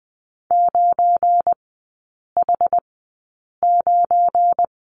Morse code, heard as a keyed tone, about 20 wpm, spaced as for 8 wpm, 700 Hz. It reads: 9H9
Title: 172 Lesson 28e - Number 9 - Three Characters 20wpm